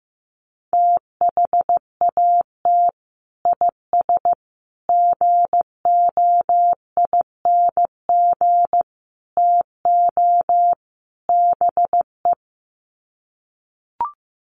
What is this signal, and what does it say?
Morse code, heard as a keyed tone, about 15 wpm, 700 Hz. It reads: THAT IS GOING TO BE
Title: Sets of 5 Words Encore2 15wpm